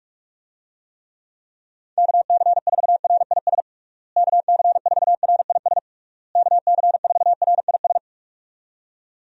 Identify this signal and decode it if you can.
Morse code, heard as a keyed tone, about 48 wpm, 700 Hz. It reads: KC4RIS KC4RIS KC4RIS
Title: USA Call Signs - 6 Characters - Repeated 3x - 45wpm